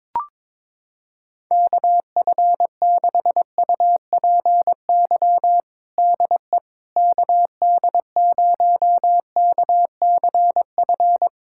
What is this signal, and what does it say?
Morse code, heard as a keyed tone, about 22 wpm, 700 Hz. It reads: KF6UPY DE KD0KCF
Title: USA Callsign DE Callsign - 6 Characters - 22wpm